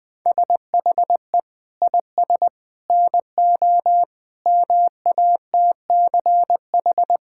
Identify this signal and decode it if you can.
Morse code, heard as a keyed tone, about 20 wpm, 700 Hz. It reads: SHE IS NO MATCH
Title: Sentences from Top 1600 Words 20wpm